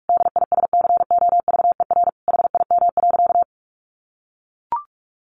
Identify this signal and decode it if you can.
Morse code, heard as a keyed tone, about 44 wpm, 700 Hz. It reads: DISCOVER HIM.
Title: Sentences from Top 1100 Words 45wpm